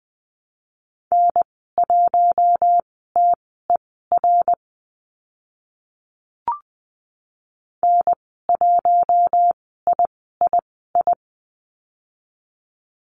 Morse code, heard as a keyed tone, about 19 wpm, spaced as for 14 wpm, 700 Hz. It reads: N1TER N1III
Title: 055 Lesson 10g - Letter r - Call Signs 20wpm